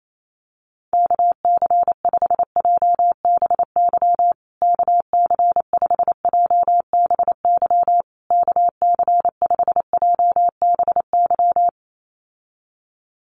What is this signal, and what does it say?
Morse code, heard as a keyed tone, about 28 wpm, 700 Hz. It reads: KC5JBY KC5JBY KC5JBY
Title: USA Call Signs Repeated 3x 28wpm